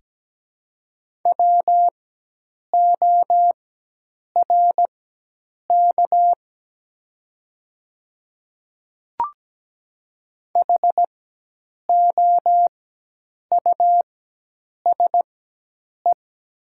Morse code, heard as a keyed tone, about 17 wpm, spaced as for 8 wpm, 700 Hz. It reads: WORK HOUSE
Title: 4x Character Spacing - Top 200 Words - 17wpm